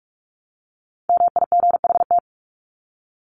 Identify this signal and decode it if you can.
Morse code, heard as a keyed tone, about 44 wpm, 700 Hz. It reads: MIGHT
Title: Top 500 Words 45wpm